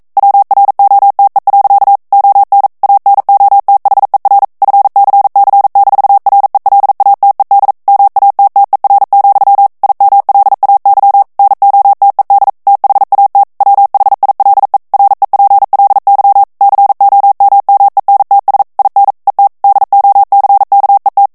Morse code, heard as a keyed tone, about 42 wpm, 800 Hz. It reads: WROTE. ON ANOTHER FCC-RELATED MATTER, IMLAY NOTED THAT WHILE REPLY COMMENTS IN ET DOCKET